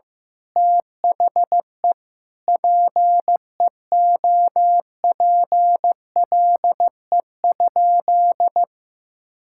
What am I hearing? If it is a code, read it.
Morse code, heard as a keyed tone, about 16 wpm, 700 Hz. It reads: THE PEOPLE?
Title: Sentences from Top 1000 Words - Review Part 5 - 15wpm